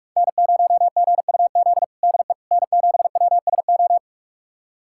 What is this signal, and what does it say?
Morse code, heard as a keyed tone, about 46 wpm, 700 Hz. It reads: N0GUZ DE N7WSO
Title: USA Callsign DE Callsign - 5 Characters - 45wpm